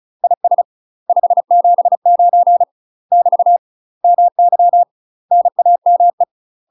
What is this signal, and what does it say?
Morse code, heard as a keyed tone, about 35 wpm, 700 Hz. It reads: IS 579 <BT> MY NAME
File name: UK MASTER QSOs TRAINING / 35WPM / UK MASTER QSOs 0005